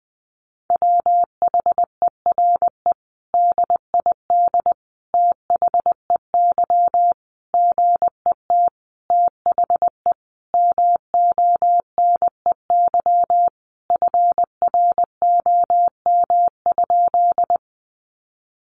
Morse code, heard as a keyed tone, about 20 wpm, 700 Hz. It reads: WHERE DID THEY GET THE MONEY FROM?
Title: Sentences from Top 400 Words 20wpm